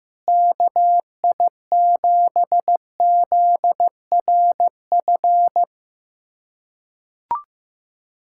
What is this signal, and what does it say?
Morse code, heard as a keyed tone, about 15 wpm, 700 Hz. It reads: KI7ZRF
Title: USA Callsign DE Callsign 15wpm